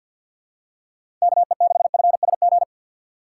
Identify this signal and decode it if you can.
Morse code, heard as a keyed tone, about 47 wpm, 700 Hz. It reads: KE6FSG
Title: USA Call Signs - 6 Characters - 6x Character Spacing - 50wpm